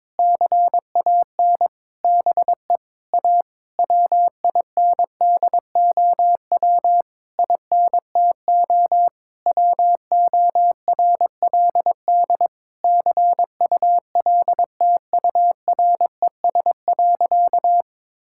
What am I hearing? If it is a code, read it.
Morse code, heard as a keyed tone, about 22 wpm, 700 Hz. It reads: CAN BE A WINDOW INTO WORLD CULTURES.
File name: Difficult Sentences with Hints - 22wpm